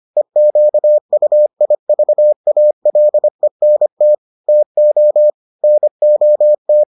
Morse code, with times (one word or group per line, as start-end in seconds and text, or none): 0.2-4.1 EQUIVALENT
4.5-5.3 TO
5.6-6.8 NOT